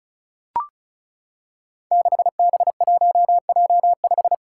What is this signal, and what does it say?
Morse code, heard as a keyed tone, about 36 wpm, 700 Hz. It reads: 6B1J5
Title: Five Challenging Letter-Number Combinations 35wpm